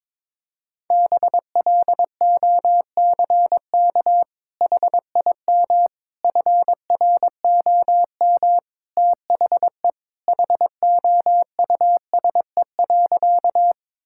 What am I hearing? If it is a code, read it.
Morse code, heard as a keyed tone, about 22 wpm, 700 Hz. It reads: BLOCK HIM FROM THE HOUSE.
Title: Sentences from Top 1600 Words 22wpm